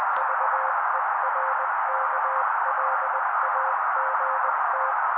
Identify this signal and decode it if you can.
Morse code, heard as a keyed tone, about 20 wpm, 530 Hz. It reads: VERKLAGT
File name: daily / 18wpm noise